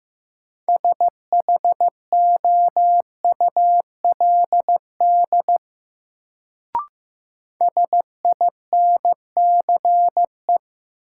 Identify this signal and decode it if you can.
Morse code, heard as a keyed tone, about 16 wpm, 700 Hz. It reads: SHOULD SINCE
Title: Sets of 2 Words - New Word Drill - 15wpm